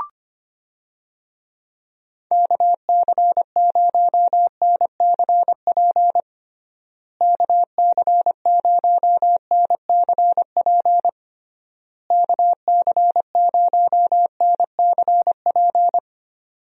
Morse code, heard as a keyed tone, about 25 wpm, 700 Hz. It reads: KC0NCP KC0NCP KC0NCP
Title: USA Call Signs - 6 Characters - Repeated 3x - 25wpm